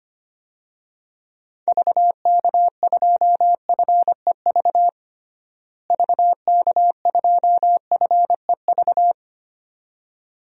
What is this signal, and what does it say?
Morse code, heard as a keyed tone, about 25 wpm, 700 Hz. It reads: VK2FEV VK2FEV
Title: DXCC-100 Call Signs - 6 Characters - Repeated 2x - 25wpm